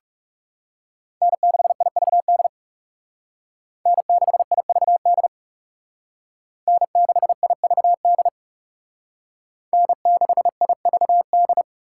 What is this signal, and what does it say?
Morse code, heard as a keyed tone, about 38 wpm, 700 Hz. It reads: N6IVD N6IVD N6IVD N6IVD